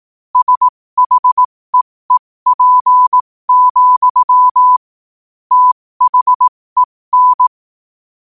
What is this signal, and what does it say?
Morse code, heard as a keyed tone, about 18 wpm, spaced as for 15 wpm, 1000 Hz. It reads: SHEEP, THEN
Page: New quotes every day in morse code at 15 Words per minute.